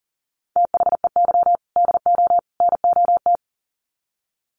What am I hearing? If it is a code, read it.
Morse code, heard as a keyed tone, about 40 wpm, 700 Hz. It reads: THEY DO NOT
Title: Sets of 3 Words 40wpm